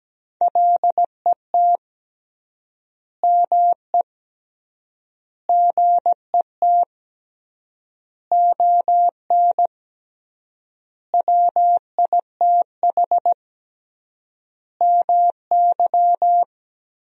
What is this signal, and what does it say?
Morse code, heard as a keyed tone, about 17 wpm, 700 Hz. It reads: LET ME GET ON WITH MY
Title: 3x Word-Spacing - Sentences from Top 300 Words - 17wpm